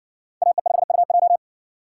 Morse code, heard as a keyed tone, about 60 wpm, 700 Hz. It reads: AE5UJ